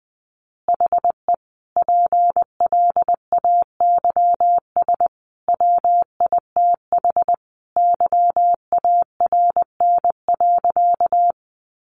Morse code, heard as a keyed tone, about 20 wpm, 700 Hz. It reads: HE PLAYS WITH YARN.
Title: Easy Phrases 20wpm